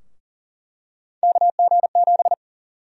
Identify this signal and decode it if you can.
Morse code, heard as a keyed tone, about 40 wpm, 700 Hz. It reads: KG7
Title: USA Call Sign Prefixes - Repeated 2x - 40wpm